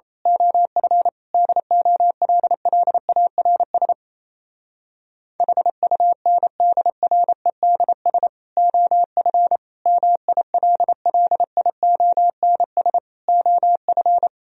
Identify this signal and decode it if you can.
Morse code, heard as a keyed tone, about 27 wpm, 700 Hz. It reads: OF DOLLARS HUNDREDS OF MILLIONS OF
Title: Speed-Racing - Sets of 5 Words 28wpm